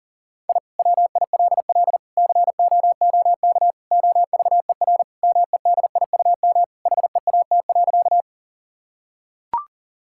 Morse code, heard as a keyed tone, about 40 wpm, 700 Hz. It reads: I WILL COOK OVER MEDIUM HEAT.
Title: Sentences from Top 1000 Words - Review Part 6 - 40wpm